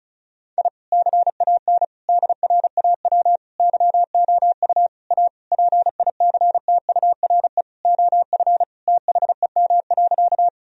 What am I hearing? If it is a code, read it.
Morse code, heard as a keyed tone, about 34 wpm, 700 Hz. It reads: I CAN DRAW YOU A PICTURE OF THEM.
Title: Sentences from Top 1000 Words - Review Part 1 - 35wpm